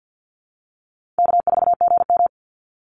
Morse code, heard as a keyed tone, about 50 wpm, 700 Hz. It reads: K4GM
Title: Call Signs - 50wpm